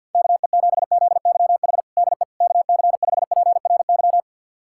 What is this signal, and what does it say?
Morse code, heard as a keyed tone, about 50 wpm, 700 Hz. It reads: KE7ZYH DE KC5PRY
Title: USA Callsign DE Callsign - 6 Characters - 50wpm